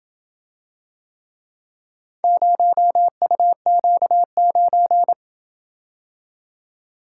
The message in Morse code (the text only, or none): 0UQ9